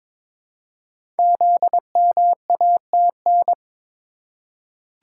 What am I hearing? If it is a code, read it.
Morse code, heard as a keyed tone, about 22 wpm, 700 Hz. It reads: ZMATN
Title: Code Groups - 5 Letters - 22wpm